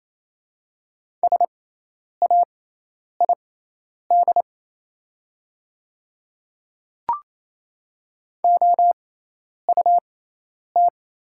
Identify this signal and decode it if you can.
Morse code, heard as a keyed tone, about 28 wpm, spaced as for 10 wpm, 700 Hz. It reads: SAID OUT
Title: 6x Character Spacing - Top 100 Words - 28wpm